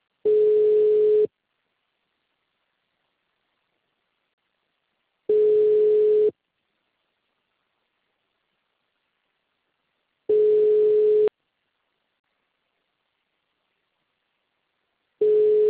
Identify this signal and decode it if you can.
Ringback tone (Europe, 425 Hz).